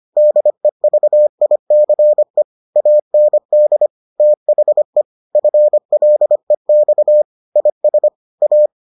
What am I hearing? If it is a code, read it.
Morse code, heard as a keyed tone, about 25 wpm, 600 Hz. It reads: DEVICE AND THE FLEX IS A